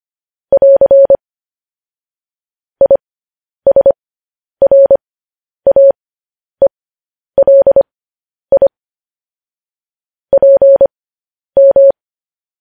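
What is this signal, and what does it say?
Morse code, heard as a keyed tone, about 25 wpm, spaced as for 10 wpm, 550 Hz. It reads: <AR> ISRAELI PM
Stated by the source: Morse code practice of news headlines at 10 wpm.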